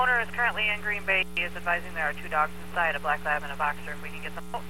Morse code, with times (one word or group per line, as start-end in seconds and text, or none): none